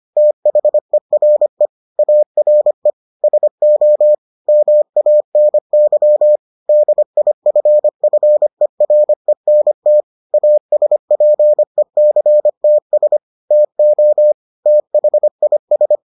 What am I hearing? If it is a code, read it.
Morse code, heard as a keyed tone, about 25 wpm, 600 Hz. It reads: THERE ARE SO MANY DIFFERENT ASPECTS TO THIS